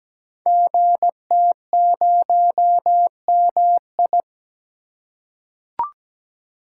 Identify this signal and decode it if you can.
Morse code, heard as a keyed tone, about 17 wpm, 700 Hz. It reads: GT0MI
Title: Code Groups - 5 Letter:Numbers - 17wpm